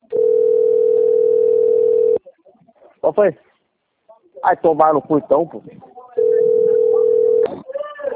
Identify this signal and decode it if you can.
Ringback tone (North America), 440 + 480 Hz, with speech.
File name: rg-1600-unknown- - - .44373